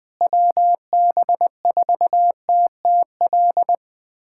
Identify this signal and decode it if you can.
Morse code, heard as a keyed tone, about 20 wpm, 700 Hz. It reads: WB4TTL
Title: USA Callsign DE Callsign - 6 Characters - 20wpm